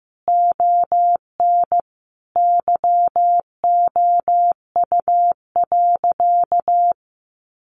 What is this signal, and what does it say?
Morse code, heard as a keyed tone, about 15 wpm, 700 Hz. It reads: ON YOU.
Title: Sentences from Top 1300 Words 15wpm